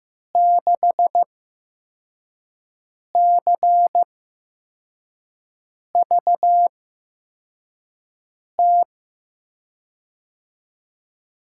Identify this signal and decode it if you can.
Morse code, heard as a keyed tone, about 15 wpm, spaced as for 4 wpm, 700 Hz. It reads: 6CVT
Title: USA Call Signs - 6 Characters - 8x Character Spacing - 15wpm